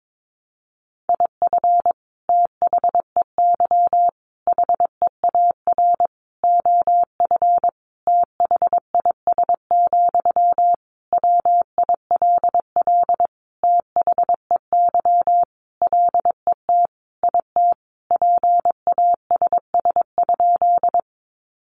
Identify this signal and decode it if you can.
Morse code, heard as a keyed tone, about 22 wpm, 700 Hz. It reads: IF THEY HEAR OF THIS, WILL THEY LET IT PASS?
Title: Sentences from Top 400 Words 22wpm